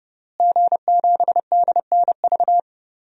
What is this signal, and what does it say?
Morse code, heard as a keyed tone, about 32 wpm, 700 Hz. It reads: G7DNV
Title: DXCC-100 Call Signs - 5 Characters - Repeated 3x - 30wpm